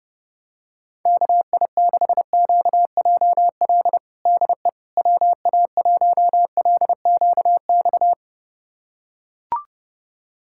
Morse code, heard as a keyed tone, about 30 wpm, 700 Hz. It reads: KI6QJL DE WA1LQX
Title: USA Callsign DE Callsign - 6 Characters - 30wpm